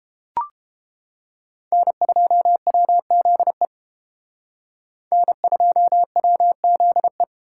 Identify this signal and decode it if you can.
Morse code, heard as a keyed tone, about 31 wpm, 700 Hz. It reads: N2WZE N2WZE